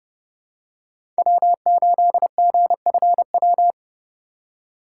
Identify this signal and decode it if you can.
Morse code, heard as a keyed tone, about 30 wpm, 700 Hz. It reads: W8GFW
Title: USA Call Signs - 5 Characters - 4x Character Spacing - 30wpm